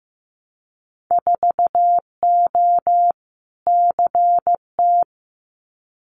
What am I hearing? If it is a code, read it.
Morse code, heard as a keyed tone, about 15 wpm, 700 Hz. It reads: 4O CT
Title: Winter Field Day Exchanges Encore 15wpm